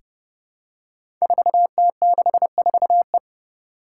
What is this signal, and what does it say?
Morse code, heard as a keyed tone, about 30 wpm, 700 Hz. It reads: 4T64E
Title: Five Challenging Letter-Number Combinations 30wpm